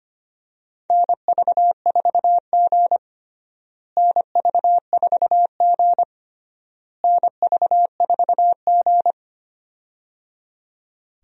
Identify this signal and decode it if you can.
Morse code, heard as a keyed tone, about 25 wpm, 700 Hz. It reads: NV4G NV4G NV4G
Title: USA Call Signs - 4 Characters - Repeated 3x - 25wpm